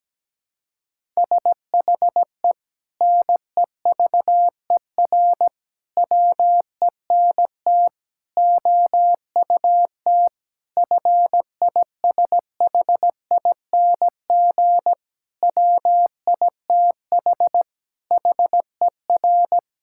Morse code, heard as a keyed tone, about 17 wpm, 700 Hz. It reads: SHE NEVER WENT OUT FISHING WITH HER
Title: Sentences from Top 1000 Words - Review Part 5 - 17wpm